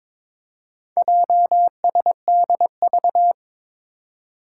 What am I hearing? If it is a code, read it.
Morse code, heard as a keyed tone, about 22 wpm, 700 Hz. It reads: JSDV
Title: Four Challenging Letter-Number Combinations 22wpm